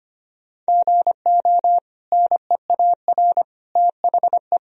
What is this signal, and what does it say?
Morse code, heard as a keyed tone, about 25 wpm, 700 Hz. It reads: GO NEAR THE
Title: Sentences from Top 1000 Words - Review Part 1 - 25wpm